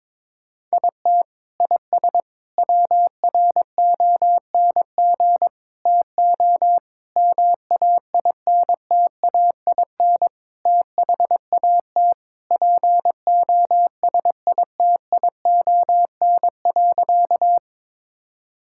Morse code, heard as a keyed tone, about 22 wpm, 700 Hz. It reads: IT IS WRONG TO MAINTAIN THAT POSITION.
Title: Sentences from Top 1000 Words - Review Part 5 - 22wpm